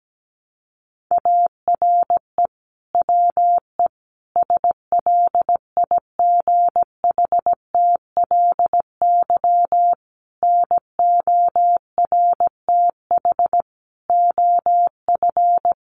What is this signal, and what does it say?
Morse code, heard as a keyed tone, about 17 wpm, 700 Hz. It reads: ARE WE SLIGHTLY NORTH OF
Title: Sentences from Top 1300 Words 17wpm